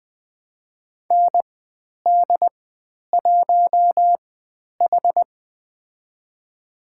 Morse code, heard as a keyed tone, about 20 wpm, spaced as for 10 wpm, 700 Hz. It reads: ND1H
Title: 102 Lesson 17g - Letter u - Call Signs 20wpm